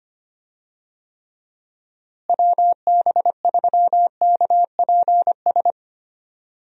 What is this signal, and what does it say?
Morse code, heard as a keyed tone, about 25 wpm, 700 Hz. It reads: WB3KPS
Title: USA Call Signs - 6 Characters - Repeated 2x - 25wpm